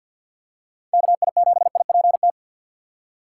Morse code, heard as a keyed tone, about 50 wpm, 700 Hz. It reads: KI7IPT
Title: USA Call Signs - 3x Character Spacing - 50wpm